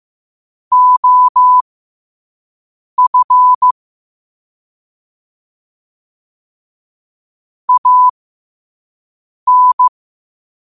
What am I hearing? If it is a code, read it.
Morse code, heard as a keyed tone, about 15 wpm, spaced as for 5 wpm, 1000 Hz. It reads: OF AN
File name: qotdfiles / 5wpm